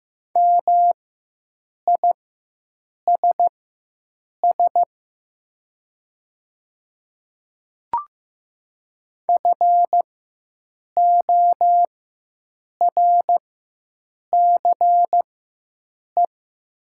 Morse code, heard as a keyed tone, about 15 wpm, spaced as for 7 wpm, 700 Hz. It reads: MISS FORCE
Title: 4x Character Spacing - Top 500 Words - 15wpm